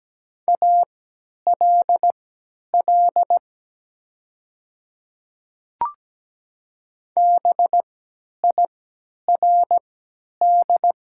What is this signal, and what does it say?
Morse code, heard as a keyed tone, about 17 wpm, spaced as for 10 wpm, 690 Hz. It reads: ALL BIRD
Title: 3x Character Spacing - Top 500 Words - 17wpm